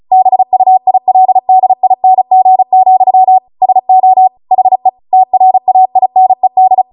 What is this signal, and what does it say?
Morse code, heard as a keyed tone, about 36 wpm, 750 Hz. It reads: BUILDING, SO HE TRAINED
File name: Morse / 35WPM